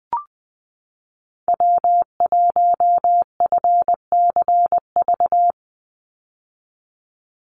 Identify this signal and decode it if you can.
Morse code, heard as a keyed tone, about 20 wpm, 700 Hz. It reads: W1FCV
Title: Call Signs - 20wpm